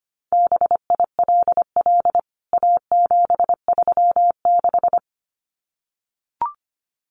Morse code, heard as a keyed tone, about 25 wpm, 700 Hz. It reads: BILL A736
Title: CWT Exchange 25wpm